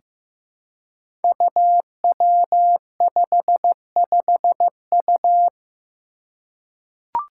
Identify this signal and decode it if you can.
Morse code, heard as a keyed tone, about 15 wpm, 700 Hz. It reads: UW55U
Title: Five Challenging Letter-Number Combinations 15wpm